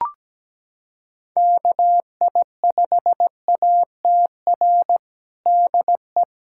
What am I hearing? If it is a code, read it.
Morse code, heard as a keyed tone, about 17 wpm, 700 Hz. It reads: KI5ATR DE
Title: USA Callsign DE Callsign 17wpm